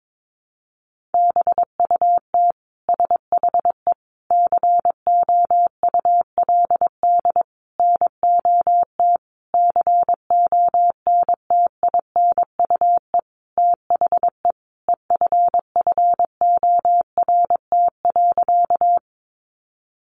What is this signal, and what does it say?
Morse code, heard as a keyed tone, about 22 wpm, 700 Hz. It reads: BUT SHE COULD NOT CONTINUE THE EFFORT.
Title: Sentences from Top 1200 Words 22wpm